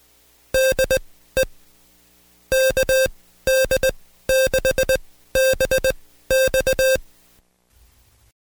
Morse code, heard as a keyed tone, about 20 wpm, spaced as for 14 wpm, 530 Hz. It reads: DE KD6BX
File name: Code Quick / AudioDownld2 / Seg133-Stepper6 13WPM